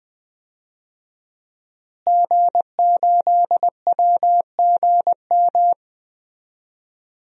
Morse code, heard as a keyed tone, about 20 wpm, 700 Hz. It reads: G8WGM
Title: DXCC-100 Call Signs - 5 Characters - Repeated 3x - 20wpm